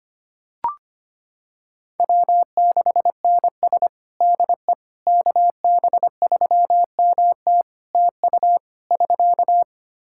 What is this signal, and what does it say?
Morse code, heard as a keyed tone, about 25 wpm, 700 Hz. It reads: W6NS DE KB3MT TU <SK>
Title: Practice QSO - Exchange 4 - Ending - 25wpm